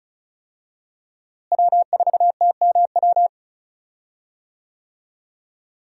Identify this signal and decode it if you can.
Morse code, heard as a keyed tone, about 36 wpm, 700 Hz. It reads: W4TMW
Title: USA Call Signs - 5 Characters - Repeated 2x - 35wpm